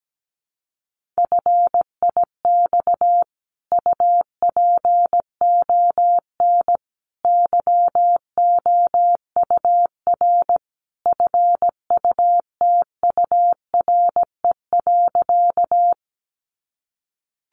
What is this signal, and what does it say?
Morse code, heard as a keyed tone, about 17 wpm, 700 Hz. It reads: FIX UPON YOUR FUTURE.
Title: Sentences from Top 1400 Words 17wpm